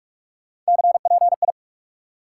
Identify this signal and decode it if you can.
Morse code, heard as a keyed tone, about 46 wpm, 700 Hz. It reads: CPI
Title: Top 90 QSO Elements 45wpm